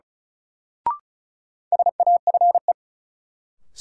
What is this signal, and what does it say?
Morse code, heard as a keyed tone, about 34 wpm, 700 Hz. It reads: SAFE
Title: Sentences from Top 900 Words - New Word Drill - 35wpm